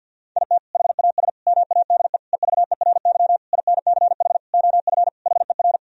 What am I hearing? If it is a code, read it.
Morse code, heard as a keyed tone, about 50 wpm, 700 Hz. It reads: IT HAS MADE EVERY INCH OF HER
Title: Sentences from Top 1000 Words - Review Part 3 - 50wpm